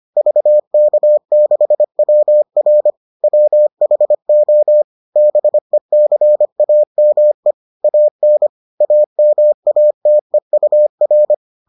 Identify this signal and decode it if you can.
Morse code, heard as a keyed tone, about 25 wpm, 600 Hz. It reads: VK6WR WHO BECAME AN AMATEUR